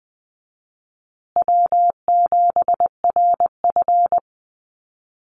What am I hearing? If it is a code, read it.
Morse code, heard as a keyed tone, about 20 wpm, 700 Hz. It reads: W7RF
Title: Easy Call Signs Repeated 2x - 20wpm